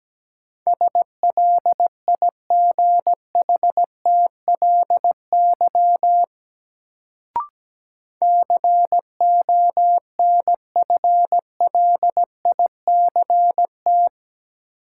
Morse code, heard as a keyed tone, about 17 wpm, 700 Hz. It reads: SLIGHTLY CONFLICT
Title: Sentences from Top 1300 Words - New Word Drill - 17wpm